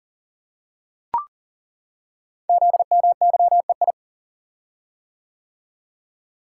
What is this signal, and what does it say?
Morse code, heard as a keyed tone, about 39 wpm, 700 Hz. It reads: ZMYEI
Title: Code Groups - 5 Letters - 40wpm